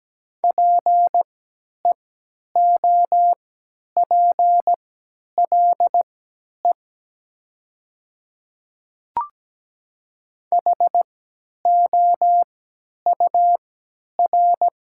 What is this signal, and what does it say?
Morse code, heard as a keyed tone, about 17 wpm, spaced as for 10 wpm, 700 Hz. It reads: PEOPLE HOUR
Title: 3x Character Spacing - Top 500 Words - 17wpm